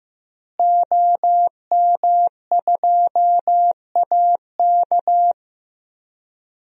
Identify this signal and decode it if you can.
Morse code, heard as a keyed tone, about 15 wpm, 700 Hz. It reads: OM2AK
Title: DXCC-100 Call Signs - 5 Characters - Repeated 2x - 15wpm